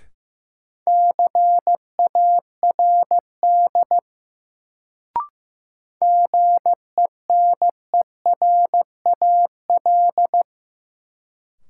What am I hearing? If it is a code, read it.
Morse code, heard as a keyed tone, about 15 wpm, 700 Hz. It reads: CARD GENERAL
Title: Sentences from Top 600 Words - New Word Drill - 15wpm